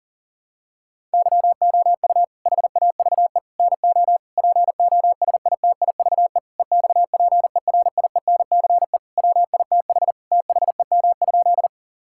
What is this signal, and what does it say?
Morse code, heard as a keyed tone, about 40 wpm, 700 Hz. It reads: YOU HAVE NO POSITIVE EXPERIENCE WITH THEM?
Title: Sentences from Top 1000 Words - Review Part 5 - 40wpm